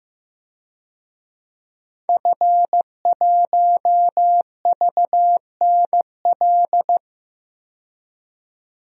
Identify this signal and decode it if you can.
Morse code, heard as a keyed tone, about 15 wpm, 700 Hz. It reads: F1VNL
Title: DXCC-100 Call Signs - 5 Characters - 15wpm